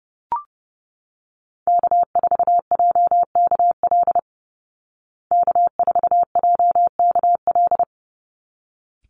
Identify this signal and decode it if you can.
Morse code, heard as a keyed tone, about 30 wpm, 700 Hz. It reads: K4JKL K4JKL
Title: Call Signs Repeated 3x - 30wpm